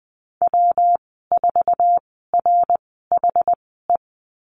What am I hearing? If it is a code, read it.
Morse code, heard as a keyed tone, about 20 wpm, spaced as for 15 wpm, 700 Hz. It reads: W4RHE
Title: 200 Lesson 32g - Letter k - Call Signs 20wpm